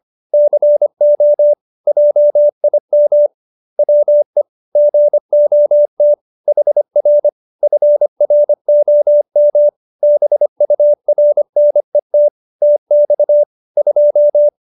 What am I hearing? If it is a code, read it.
Morse code, heard as a keyed tone, about 25 wpm, 590 Hz. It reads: CO JIM WE GOT HR FROM BURNET TX 2
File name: QSO312 25